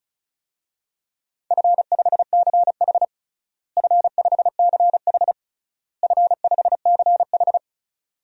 Morse code, heard as a keyed tone, about 34 wpm, 700 Hz. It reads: F5CH F5CH F5CH
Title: DXCC-100 Call Signs - 4 Characters - Repeated 3x - 35wpm